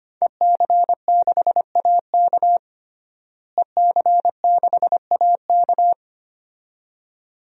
Morse code, heard as a keyed tone, about 25 wpm, 700 Hz. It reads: EC6AK EC6AK
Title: DXCC-100 Call Signs - 5 Characters - Repeated 2x - 25wpm